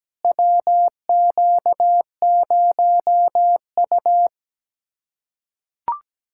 Morse code, heard as a keyed tone, about 17 wpm, 700 Hz. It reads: WQ0U
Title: USA Callsign DE Callsign - 4 Characters - 17wpm